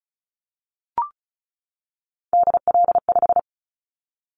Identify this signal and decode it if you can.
Morse code, heard as a keyed tone, about 36 wpm, 700 Hz. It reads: DL5
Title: Callsign Prefixes for Top 50 DXCC - 35wpm